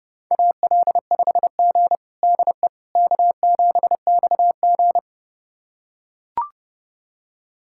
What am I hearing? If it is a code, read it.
Morse code, heard as a keyed tone, about 30 wpm, 700 Hz. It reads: AL5G DE K7XG
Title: USA Callsign DE Callsign - 4 Characters - 30wpm